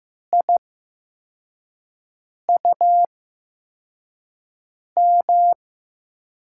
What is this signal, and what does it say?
Morse code, heard as a keyed tone, about 15 wpm, spaced as for 4 wpm, 700 Hz. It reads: IUM
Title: USA Call Signs - 6 Characters - 8x Character Spacing - 15wpm